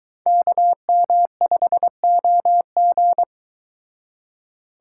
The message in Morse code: KM5OG

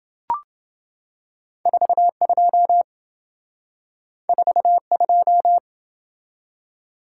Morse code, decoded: 42 42

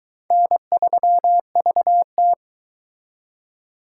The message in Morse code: N3VT

